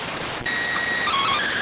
SSTV (slow-scan television), PD 290.